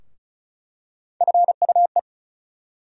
Morse code, heard as a keyed tone, about 36 wpm, 700 Hz. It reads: FUE
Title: USA Call Sign Postfixes - 3 Letters 35wpm